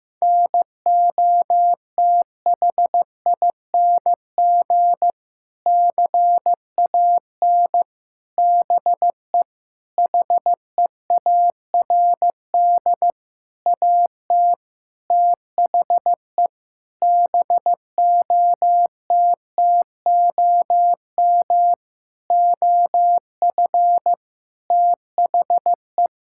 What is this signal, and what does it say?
Morse code, heard as a keyed tone, about 15 wpm, 700 Hz. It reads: NOTHING CAN BE HEARD AT THE BOTTOM OF THE
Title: Sentences from Top 1000 Words - Review Part 6 - 15wpm